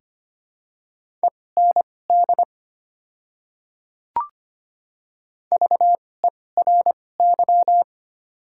Morse code, heard as a keyed tone, about 25 wpm, spaced as for 18 wpm, 700 Hz. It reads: END VERY
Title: 2x Character Spacing - Top 300 Words - 25wpm